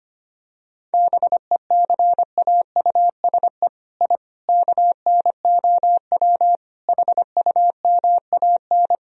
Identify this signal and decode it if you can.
Morse code, heard as a keyed tone, about 25 wpm, 700 Hz. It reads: BECAUSE I KNOW HUMAN
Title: Sentences from Top 1000 Words - Review Part 3 - 25wpm